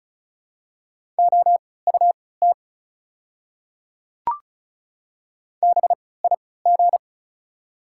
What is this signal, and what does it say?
Morse code, heard as a keyed tone, about 36 wpm, spaced as for 20 wpm, 690 Hz. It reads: OUT BIG
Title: 3x Character Spacing - Top 200 Words - 35wpm